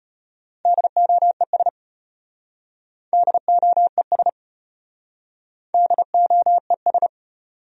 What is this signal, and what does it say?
Morse code, heard as a keyed tone, about 34 wpm, 700 Hz. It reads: DOES DOES DOES